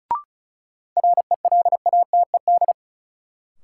Morse code, heard as a keyed tone, about 36 wpm, 700 Hz. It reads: RELATED
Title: Sentences from Top 1000 Words - Review Part 5 - New Word Drill - 35wpm